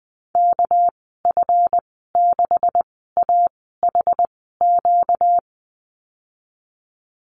Morse code, heard as a keyed tone, about 20 wpm, spaced as for 14 wpm, 700 Hz. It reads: KF6AHQ
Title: 245 Lesson 39g - Letter z - Call Signs 20wpm